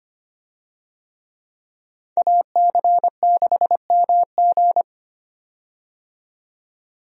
Morse code, heard as a keyed tone, about 25 wpm, 700 Hz. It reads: AC6MG
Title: USA Call Signs - 5 Characters - 25wpm